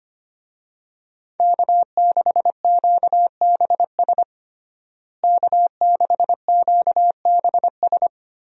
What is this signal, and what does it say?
Morse code, heard as a keyed tone, about 25 wpm, 700 Hz. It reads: K6QBS K6QBS
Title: USA Call Signs - 5 Characters - Repeated 2x - 25wpm